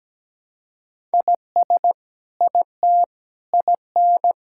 Morse code, heard as a keyed tone, about 17 wpm, 700 Hz. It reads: IS IT IN
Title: Sentences from Top 1000 Words - Review Part 6 - 17wpm